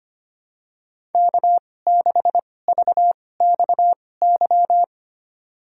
Morse code, heard as a keyed tone, about 25 wpm, spaced as for 18 wpm, 700 Hz. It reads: K6VXY
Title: USA Call Signs - 5 Characters - 2x Character Spacing - 25wpm